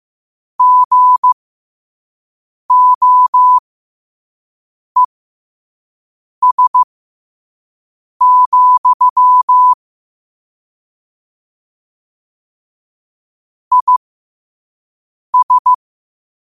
Morse code, heard as a keyed tone, about 15 wpm, spaced as for 5 wpm, 1000 Hz. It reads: GOES, IS
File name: qotdfiles / 5wpm